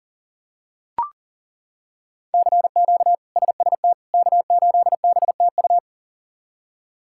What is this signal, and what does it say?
Morse code, heard as a keyed tone, about 40 wpm, 700 Hz. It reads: CQ SST K8BTU
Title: K1USN SST QSO Practice - 40wpm